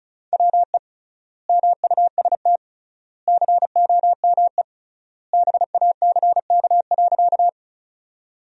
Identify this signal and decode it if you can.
Morse code, heard as a keyed tone, about 36 wpm, 700 Hz. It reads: WE MUST COME BACK.
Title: 3x Word-Spacing - Sentences from Top 200 Words - 35wpm